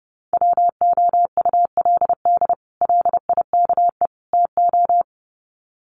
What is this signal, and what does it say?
Morse code, heard as a keyed tone, about 28 wpm, 700 Hz. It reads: WOULD LIKE TO
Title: Sets of 3 Words 30wpm